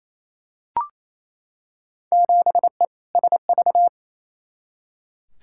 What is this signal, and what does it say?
Morse code, heard as a keyed tone, about 28 wpm, 700 Hz. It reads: 7E SV
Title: Field Day Exchanges Encore - 28wpm